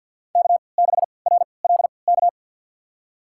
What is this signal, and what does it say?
Morse code, heard as a keyed tone, about 53 wpm, spaced as for 29 wpm, 700 Hz. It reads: K6RLK
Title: USA Call Signs - 5 Characters - 3x Character Spacing - 50wpm